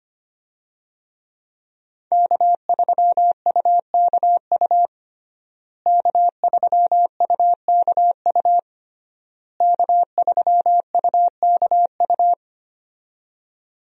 Morse code, heard as a keyed tone, about 25 wpm, 700 Hz. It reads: K3UKU K3UKU K3UKU
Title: USA Call Signs - 5 Characters - Repeated 3x - 25wpm